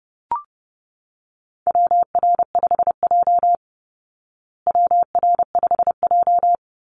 Morse code, heard as a keyed tone, about 30 wpm, 700 Hz. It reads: WR5J WR5J
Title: Call Signs Repeated 2x - 30wpm